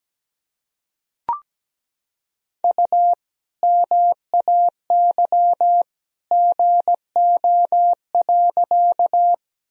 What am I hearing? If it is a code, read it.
Morse code, heard as a keyed tone, about 17 wpm, 700 Hz. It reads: U MAY GO.
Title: Sentences from Top 100 Words and Abbreviations 17wpm